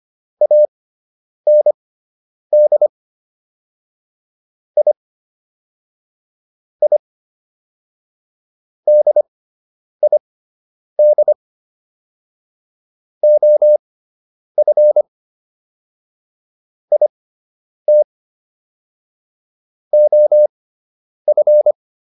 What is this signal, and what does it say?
Morse code, heard as a keyed tone, about 25 wpm, spaced as for 9 wpm, 600 Hz. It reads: AND I I DID OF IT OF